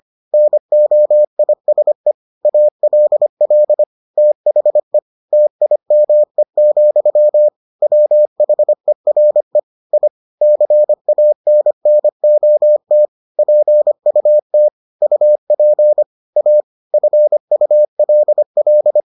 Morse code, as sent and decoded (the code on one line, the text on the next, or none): -. --- .. ... / .- .-.. .-.. / - .... / - .. -- . --..-- / .-- .... .-. / .. / -.-. .- -. -. --- - / .--. ..- - / ..- .--. / .- / ..-. ..- .-.. .-..
NOISE ALL THE TIME, WHERE I CANNOT PUT UP A FULL